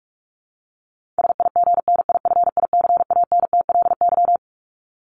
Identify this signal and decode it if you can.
Morse code, heard as a keyed tone, about 44 wpm, 700 Hz. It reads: SIGNIFICANTLY
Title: Top 1251-1500 Words 45wpm